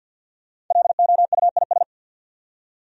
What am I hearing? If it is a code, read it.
Morse code, heard as a keyed tone, about 46 wpm, 700 Hz. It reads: LOUIS
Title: USA Common Names - Max 5 Letters - 50wpm